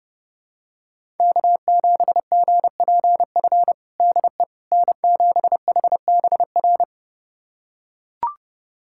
Morse code, heard as a keyed tone, about 32 wpm, 700 Hz. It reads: K7GPF DE N7HBR
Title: USA Callsign DE Callsign - 5 Characters - 30wpm